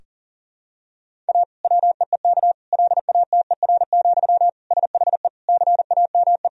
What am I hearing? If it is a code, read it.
Morse code, heard as a keyed tone, about 40 wpm, 700 Hz. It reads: A WEEK LATER, SHE CAME
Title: Sentences from Top 1000 Words - Review Part 4 - 40wpm